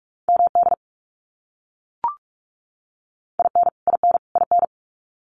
Morse code, heard as a keyed tone, about 44 wpm, 700 Hz. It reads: MD IN IN IN
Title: US States Repeated 3x 45wpm